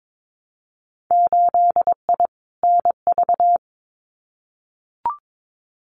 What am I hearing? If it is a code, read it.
Morse code, heard as a keyed tone, about 22 wpm, 700 Hz. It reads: 8I NV
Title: Winter Field Day Exchanges Encore 22wpm